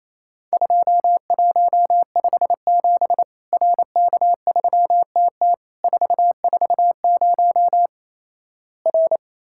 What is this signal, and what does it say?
Morse code, heard as a keyed tone, about 28 wpm, 700 Hz. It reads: 2157 RK3TT 440 R
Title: Worked All Europe DX Contest - QTC Exchange - 28wpm